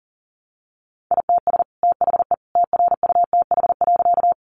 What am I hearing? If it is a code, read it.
Morse code, heard as a keyed tone, about 40 wpm, 700 Hz. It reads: ITS THE TRUTH.
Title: Sentences from Top 800 Words 40wpm